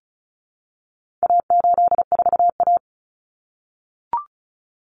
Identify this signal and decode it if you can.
Morse code, heard as a keyed tone, about 34 wpm, 700 Hz. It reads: A84A
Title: Common Cut Numbers - 4-Digit 35wpm